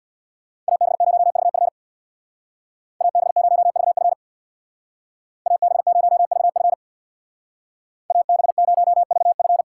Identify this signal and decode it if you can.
Morse code, heard as a keyed tone, about 60 wpm, 700 Hz. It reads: AB0VF AB0VF AB0VF AB0VF
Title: Speed Racing - USA 2x2 Call Signs 50wpm